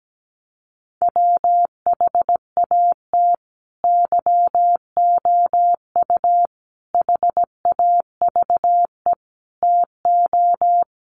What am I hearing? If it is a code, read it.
Morse code, heard as a keyed tone, about 17 wpm, 700 Hz. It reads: WHAT YOU HAVE TO
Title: Sets of 5 Words Encore 17wpm